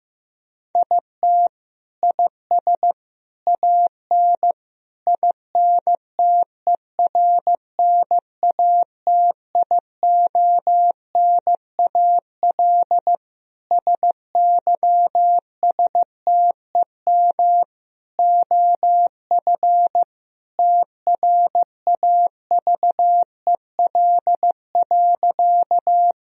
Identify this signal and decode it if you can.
Morse code, heard as a keyed tone, about 15 wpm, 700 Hz. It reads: IT IS AN INTERNATIONAL SYSTEM OF TRAVEL.
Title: Sentences from Top 1000 Words - Review Part 4 - 15wpm